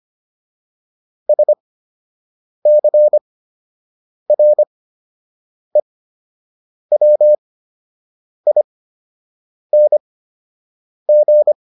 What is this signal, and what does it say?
Morse code, heard as a keyed tone, about 25 wpm, spaced as for 7 wpm, 600 Hz. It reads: SCREWING